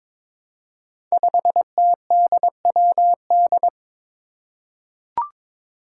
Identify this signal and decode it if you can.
Morse code, heard as a keyed tone, about 22 wpm, 700 Hz. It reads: 5TDWD
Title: Five Challenging Letter-Number Combinations 22wpm